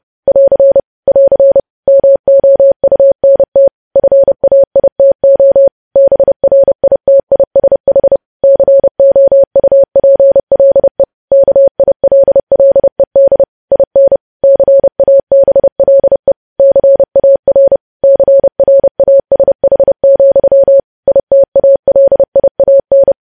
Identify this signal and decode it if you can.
Morse code, heard as a keyed tone, about 30 wpm, 550 Hz. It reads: <AR> <AR> MOUNT FAITO BRITISH COUPLE KILLED IN CABLE CAR CRASH, ITALIAN
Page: news headlines in morse code at 30 wpm